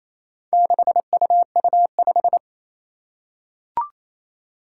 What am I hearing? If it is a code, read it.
Morse code, heard as a keyed tone, about 28 wpm, 700 Hz. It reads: 6UU5
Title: Four Challenging Letter-Number Combinations 28wpm